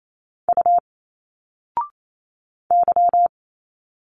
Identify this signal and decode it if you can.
Morse code, heard as a keyed tone, about 28 wpm, 700 Hz. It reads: U Y
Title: Single Letters 28wpm